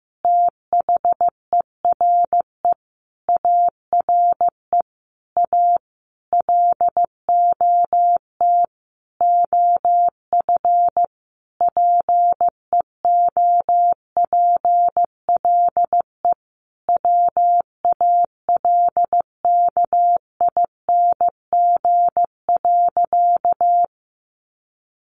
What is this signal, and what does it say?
Morse code, heard as a keyed tone, about 14 wpm, 700 Hz. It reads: THERE ARE A LOT OF PEOPLE WALKING.
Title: Easy Phrases 15wpm